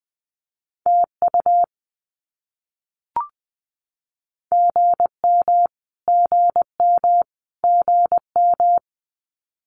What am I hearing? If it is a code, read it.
Morse code, heard as a keyed tone, about 20 wpm, 700 Hz. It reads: TU GM GM GM
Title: Abbreviations Repeated 3x - 20wpm